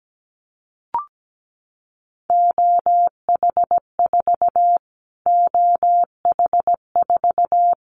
Morse code, heard as a keyed tone, about 17 wpm, 700 Hz. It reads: OH4 OH4
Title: Callsign Prefixes for Top 25 DXCC - Repeated 3x - 17wpm